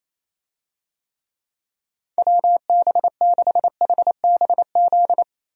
Morse code, heard as a keyed tone, about 28 wpm, 700 Hz. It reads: WB6HBZ
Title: USA Call Signs - 6 Characters - 28wpm